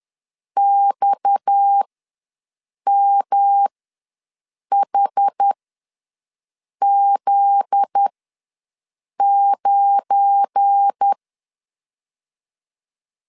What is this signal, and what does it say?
Morse code, heard as a keyed tone, about 10 wpm, spaced as for 6 wpm, 800 Hz. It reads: XMHZ9
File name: LEZIONI / 9 2 ws 10wpm 4130